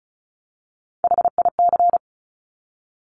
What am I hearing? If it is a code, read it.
Morse code, heard as a keyed tone, about 36 wpm, 690 Hz. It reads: HIC
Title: Top Three Letter Combinations in Words 35wpm